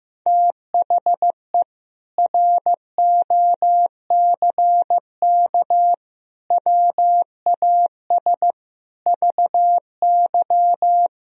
Morse code, heard as a keyed tone, about 15 wpm, 700 Hz. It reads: THE ROCK WAS VY
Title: Sentences from Top 400 Words and Abbreviations 15wpm